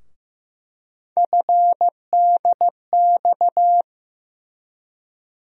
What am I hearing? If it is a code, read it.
Morse code, heard as a keyed tone, about 16 wpm, 700 Hz. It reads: FDX
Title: USA Call Sign Postfixes - 3 Letters 15wpm